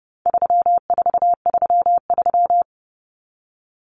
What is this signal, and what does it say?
Morse code, heard as a keyed tone, about 30 wpm, 700 Hz. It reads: 3433
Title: Numbers - 4-Digit 30wpm